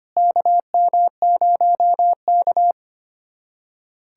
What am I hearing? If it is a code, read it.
Morse code, heard as a keyed tone, about 25 wpm, 700 Hz. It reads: KM0K
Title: USA Callsign DE Callsign - 4 Characters - 25wpm